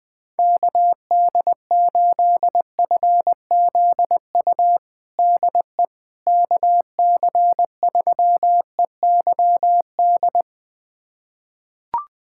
Morse code, heard as a keyed tone, about 21 wpm, 700 Hz. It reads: KD8FZU DE KC3EYD
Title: USA Callsign DE Callsign 20wpm